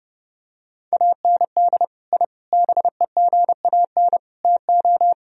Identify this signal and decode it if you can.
Morse code, heard as a keyed tone, about 32 wpm, 700 Hz. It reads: AND I BEGAN TO
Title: Sentences from Top 1000 Words - Review Part 2 - 30wpm